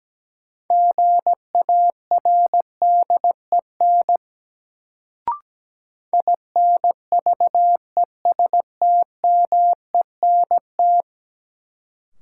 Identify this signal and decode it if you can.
Morse code, heard as a keyed tone, about 17 wpm, 700 Hz. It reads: GARDEN INVESTMENT
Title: Sentences from Top 700 Words - New Word Drill - 17wpm